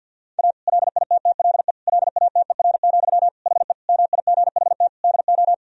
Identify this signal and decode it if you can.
Morse code, heard as a keyed tone, about 50 wpm, 700 Hz. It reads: A LITTLE LATER, HE MIGHT DO
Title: Sentences from Top 1000 Words - Review Part 4 - 50wpm